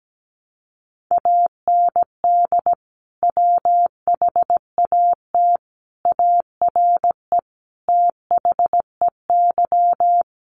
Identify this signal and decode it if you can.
Morse code, heard as a keyed tone, about 17 wpm, 700 Hz. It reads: AND WHAT ARE THEY
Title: Sentences from Top 800 Words 17wpm